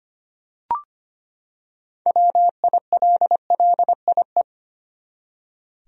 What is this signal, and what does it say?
Morse code, heard as a keyed tone, about 25 wpm, 700 Hz. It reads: WILLIE
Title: USA Common Names - 25wpm